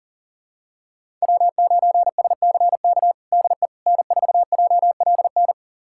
Morse code, heard as a keyed tone, about 40 wpm, 700 Hz. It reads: W9SCK DE N4JLN
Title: USA Callsign DE Callsign - 5 Characters - 40wpm